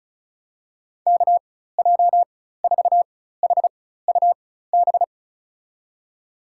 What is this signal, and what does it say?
Morse code, heard as a keyed tone, about 36 wpm, spaced as for 16 wpm, 700 Hz. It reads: KJ4HUB
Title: USA Call Signs - 4x Character Spacing - 35wpm